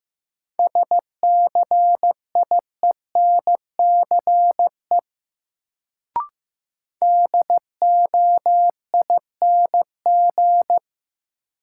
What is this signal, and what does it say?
Morse code, heard as a keyed tone, about 15 wpm, 700 Hz. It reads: SCIENCE DOING
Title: Sentences from Top 1400 Words - New Word Drill Alt - 15wpm